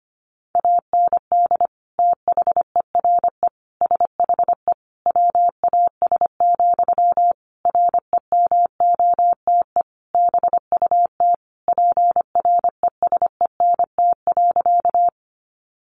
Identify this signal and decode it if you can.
Morse code, heard as a keyed tone, about 25 wpm, 700 Hz. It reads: AND THERE SHE WAS, REMOTE BUT PRESENT.
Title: Sentences from Top 1300 Words 25wpm